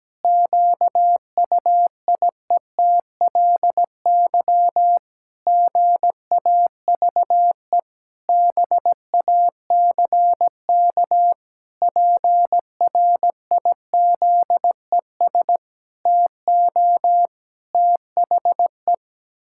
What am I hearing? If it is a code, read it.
Morse code, heard as a keyed tone, about 17 wpm, 700 Hz. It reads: QUIETLY GAVE BACK PRIZES TO THE
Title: Pangram Challenge v2 - 17wpm